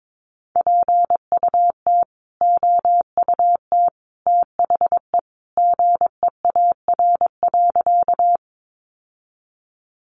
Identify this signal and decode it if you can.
Morse code, heard as a keyed tone, about 22 wpm, 700 Hz. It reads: PUT OUT THE GEAR.
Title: Sentences from Top 1400 Words 22wpm